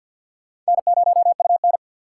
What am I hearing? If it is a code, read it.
Morse code, heard as a keyed tone, about 51 wpm, 700 Hz. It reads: N0UN
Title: USA Call Signs - 4 Characters - 2x Character Spacing - 50wpm